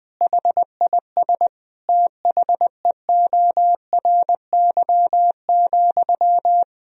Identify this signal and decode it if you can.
Morse code, heard as a keyed tone, about 21 wpm, 700 Hz. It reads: HIS THEORY,
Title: Sentences from Top 1000 Words - Review Part 4 - 20wpm